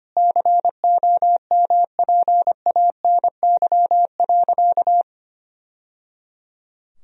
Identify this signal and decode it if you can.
Morse code, heard as a keyed tone, about 25 wpm, 700 Hz. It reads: COMPANY.
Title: Sentences from Top 1000 Words - Review Part 3 - 25wpm